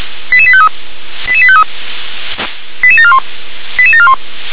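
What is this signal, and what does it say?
Five-tone selective call (ZVEI), heard as quick runs of five tones, tones of about 70 ms. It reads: 88853 88853 88741 88741
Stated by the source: Five-tone ZVEI selective call 88771 received by SDR.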